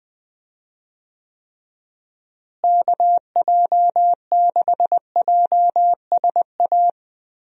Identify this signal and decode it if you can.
Morse code, heard as a keyed tone, about 20 wpm, 700 Hz. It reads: KJ6JSA
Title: USA Call Signs - 6 Characters - Repeated 3x - 20wpm